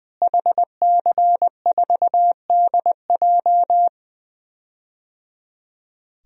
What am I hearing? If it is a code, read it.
Morse code, heard as a keyed tone, about 20 wpm, 700 Hz. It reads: HC4DJ
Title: DXCC-100 Call Signs - 5 Characters - Repeated 3x - 20wpm